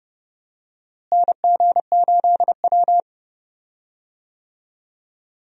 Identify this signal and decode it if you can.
Morse code, heard as a keyed tone, about 30 wpm, 700 Hz. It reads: NG8W
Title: USA Call Signs - 4 Characters - Repeated 2x - 30wpm